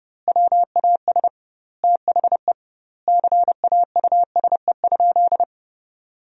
Morse code, heard as a keyed tone, about 30 wpm, 700 Hz. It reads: WAS THE CAUSE?
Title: 2x Word-Spacing - Sentences from Top 200 Words - 30wpm